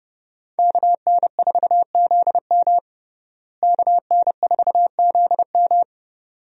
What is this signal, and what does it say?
Morse code, heard as a keyed tone, about 30 wpm, 700 Hz. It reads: KN4ZM KN4ZM
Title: USA Call Signs - 5 Characters - Repeated 2x - 30wpm